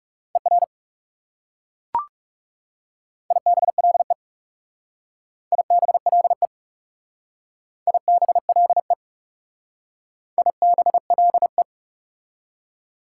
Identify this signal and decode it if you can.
Morse code, heard as a keyed tone, about 40 wpm, 700 Hz. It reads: ER IBLE IBLE IBLE IBLE